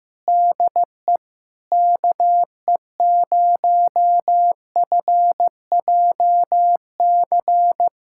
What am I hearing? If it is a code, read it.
Morse code, heard as a keyed tone, about 16 wpm, 700 Hz. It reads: DE KE0FJC
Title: USA Callsign DE Callsign - 6 Characters - 15wpm